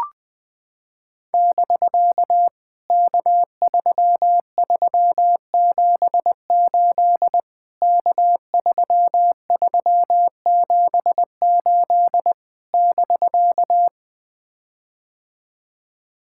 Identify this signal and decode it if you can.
Morse code, heard as a keyed tone, about 20 wpm, 700 Hz. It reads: <BK> K3378 K3378 <BK>
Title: POTA QSO Practice - 20wpm